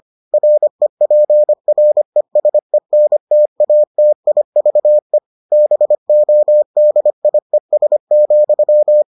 Morse code, heard as a keyed tone, about 25 wpm, 600 Hz. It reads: REPRESENTATIVE BODIES,